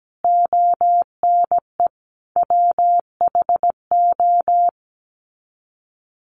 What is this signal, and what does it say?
Morse code, heard as a keyed tone, about 17 wpm, 700 Hz. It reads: ONE WHO
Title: Sets of 5 Words Encore2 17wpm